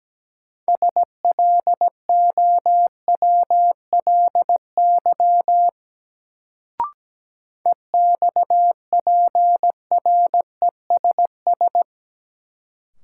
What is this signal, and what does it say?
Morse code, heard as a keyed tone, about 17 wpm, 700 Hz. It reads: SLOWLY EXPRESS
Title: Sentences from Top 1100 Words - New Word Drill Alt - 17wpm